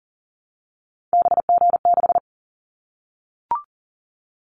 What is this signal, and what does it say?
Morse code, heard as a keyed tone, about 40 wpm, 700 Hz. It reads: BG6
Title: Callsign Prefixes for Top 100 DXCC - 40wpm